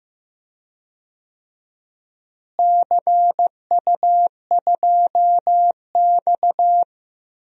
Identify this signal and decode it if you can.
Morse code, heard as a keyed tone, about 15 wpm, 700 Hz. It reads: CU2X
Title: DXCC-100 Call Signs - 4 Characters - Repeated 3x - 15wpm